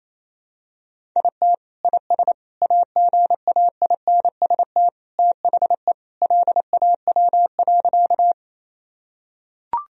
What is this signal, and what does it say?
Morse code, heard as a keyed tone, about 28 wpm, 700 Hz. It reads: IT IS AGAINST THE LAW.
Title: Sentences from Top 1000 Words - Review Part 3 - 28wpm